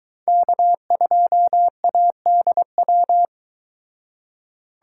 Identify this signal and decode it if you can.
Morse code, heard as a keyed tone, about 22 wpm, 700 Hz. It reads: K2ADW